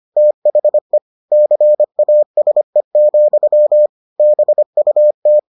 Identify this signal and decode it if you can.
Morse code, heard as a keyed tone, about 25 wpm, 600 Hz. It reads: THE CASE, BUT